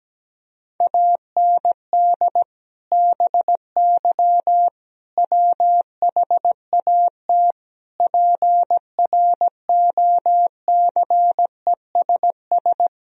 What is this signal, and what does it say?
Morse code, heard as a keyed tone, about 17 wpm, 700 Hz. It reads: AND BY WHAT PROCESS
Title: Sentences from Top 1000 Words - Review Part 3 - 17wpm